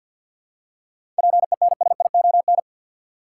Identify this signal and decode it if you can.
Morse code, heard as a keyed tone, about 51 wpm, 700 Hz. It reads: PENSION
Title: Seven Letter Words - 50wpm